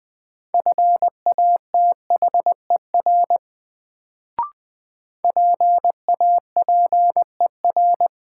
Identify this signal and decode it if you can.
Morse code, heard as a keyed tone, about 20 wpm, 700 Hz. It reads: FATHER PAPER
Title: Sentences from Top 300 Words - New Word Drill - 20wpm